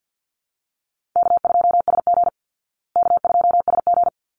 Morse code, heard as a keyed tone, about 50 wpm, 700 Hz. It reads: K2SG K2SG
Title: Call Signs Repeated 3x - 50wpm